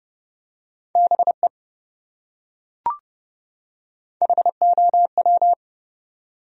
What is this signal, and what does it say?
Morse code, heard as a keyed tone, about 28 wpm, 700 Hz. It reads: BE HOW